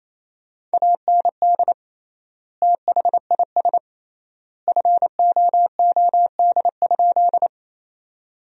Morse code, heard as a keyed tone, about 28 wpm, 700 Hz. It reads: AND THIS FOOD?
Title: 3x Word-Spacing - Sentences from Top 300 Words - 28wpm